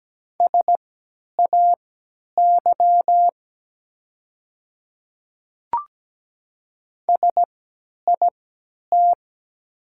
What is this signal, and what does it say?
Morse code, heard as a keyed tone, about 17 wpm, spaced as for 10 wpm, 700 Hz. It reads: SAY SIT
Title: 3x Character Spacing - Top 500 Words - 17wpm